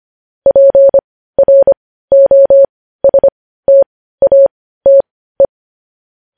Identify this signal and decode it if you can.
Morse code, heard as a keyed tone, about 25 wpm, spaced as for 15 wpm, 550 Hz. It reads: PROSTATE
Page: news-headlines-in-morse-code-at-15-wpm